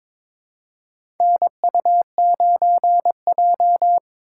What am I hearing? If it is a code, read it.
Morse code, heard as a keyed tone, about 22 wpm, 700 Hz. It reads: NU9J
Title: USA Call Signs - 4 Characters - 2x Character Spacing - 22wpm